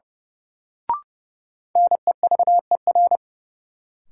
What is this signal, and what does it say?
Morse code, heard as a keyed tone, about 30 wpm, 700 Hz. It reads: NEVER
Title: Sentences from Top 300 Words - New Word Drill - 30wpm